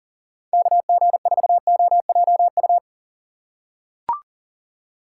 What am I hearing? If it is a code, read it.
Morse code, heard as a keyed tone, about 40 wpm, 700 Hz. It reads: KG4OJU
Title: USA Call Signs - 6x Character Spacing - 40wpm